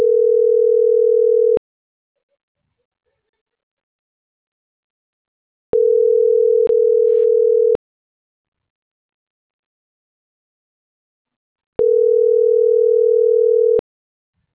Ringback tone (North America), 440 + 480 Hz.